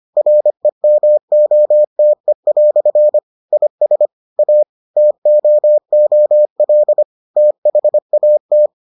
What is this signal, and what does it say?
Morse code, heard as a keyed tone, about 25 wpm, 600 Hz. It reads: REMOTE" IS A TOOL THAT